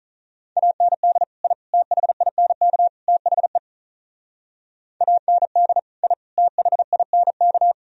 Morse code, heard as a keyed tone, about 46 wpm, 700 Hz. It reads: AND I THINK THE AND I THINK